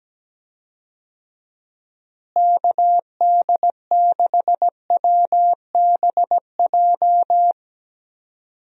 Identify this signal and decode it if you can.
Morse code, heard as a keyed tone, about 17 wpm, 700 Hz. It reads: KD6WBJ
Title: USA Call Signs - 6 Characters - Repeated 3x - 17wpm